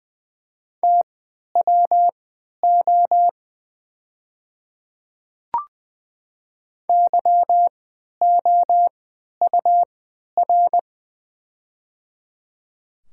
Morse code, heard as a keyed tone, about 20 wpm, spaced as for 11 wpm, 700 Hz. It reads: TWO YOUR
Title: 3x Character Spacing - Top 200 Words - 20wpm